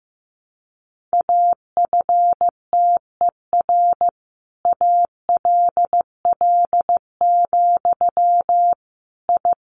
Morse code, heard as a keyed tone, about 15 wpm, 700 Hz. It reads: AFTER ALL, I
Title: Sentences from Top 600 Words 15wpm